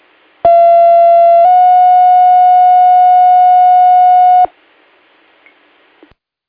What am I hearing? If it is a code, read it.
Two-tone page, heard as a one-second tone, then a three-second tone, about 680 then 730 Hz. It is heard from 0.4 to 4.5 s.